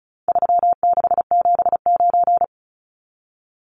Morse code, heard as a keyed tone, about 34 wpm, 700 Hz. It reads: 3679
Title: Numbers - 4-Digit 35wpm